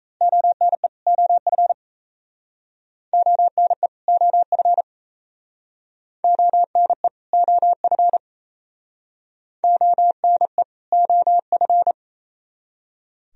Morse code, heard as a keyed tone, about 35 wpm, 690 Hz. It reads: ONE OF ONE OF ONE OF ONE OF